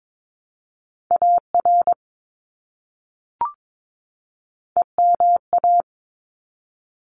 Morse code, heard as a keyed tone, about 22 wpm, 700 Hz. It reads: AR EMA
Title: ARRL Sections 22wpm